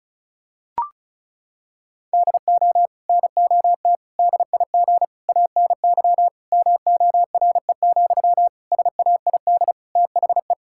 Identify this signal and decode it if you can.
Morse code, heard as a keyed tone, about 34 wpm, 700 Hz. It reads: DO NOT DIG ANY MORE, SAID THE
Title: Sentences from Top 1600 Words 35wpm